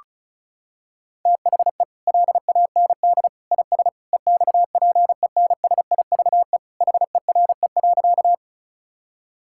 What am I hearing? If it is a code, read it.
Morse code, heard as a keyed tone, about 36 wpm, 700 Hz. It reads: THE LAND IS EXPENSIVE HERE.
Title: Sentences from Top 1000 Words - Review Part 6 - 35wpm